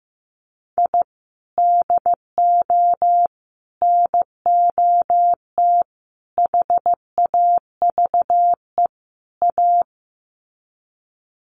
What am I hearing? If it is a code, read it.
Morse code, heard as a keyed tone, about 15 wpm, 700 Hz. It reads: I DO NOT HAVE A
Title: Sets of 5 Words 15wpm